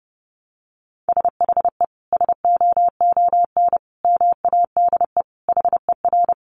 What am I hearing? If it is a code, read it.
Morse code, heard as a keyed tone, about 30 wpm, 700 Hz. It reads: SHE SOON MADE HER
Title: Sentences from Top 1100 Words 30wpm